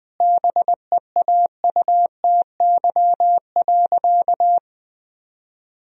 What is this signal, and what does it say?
Morse code, heard as a keyed tone, about 20 wpm, 700 Hz. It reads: BEAUTY.
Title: Sentences from Top 1000 Words - Review Part 3 - 20wpm